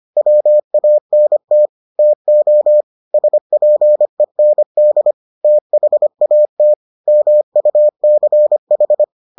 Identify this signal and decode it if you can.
Morse code, heard as a keyed tone, about 25 wpm, 600 Hz. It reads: WANT TO SPEND THAT MUCH